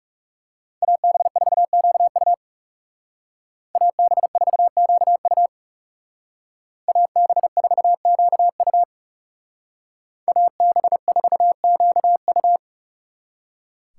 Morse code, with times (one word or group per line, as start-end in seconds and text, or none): 0.8-2.3 AB4QU
3.8-5.5 AB4QU
6.9-8.8 AB4QU
10.3-12.6 AB4QU